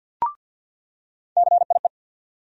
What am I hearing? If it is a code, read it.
Morse code, heard as a keyed tone, about 53 wpm, 700 Hz. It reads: CIE
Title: USA Call Sign Postfixes - 3 Letters 50wpm